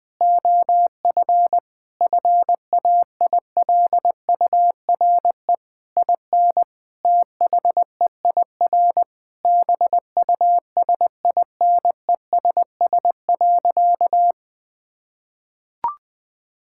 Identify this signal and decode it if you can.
Morse code, heard as a keyed tone, about 20 wpm, 700 Hz. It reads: OF FAILURE IN THEIR BUSINESS.
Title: Sentences from Top 1000 Words - Review Part 6 - 20wpm